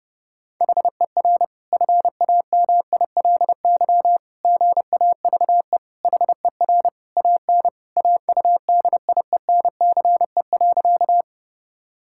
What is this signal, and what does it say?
Morse code, heard as a keyed tone, about 30 wpm, 700 Hz. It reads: HER FAMILY GAVE HER AN AUDIENCE.
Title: Sentences from Top 1000 Words - Review Part 5 - 30wpm